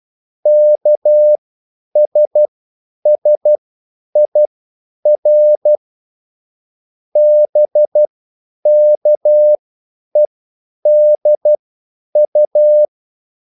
Morse code, heard as a keyed tone, about 12 wpm, spaced as for 9 wpm, 600 Hz. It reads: KSSIR BKEDU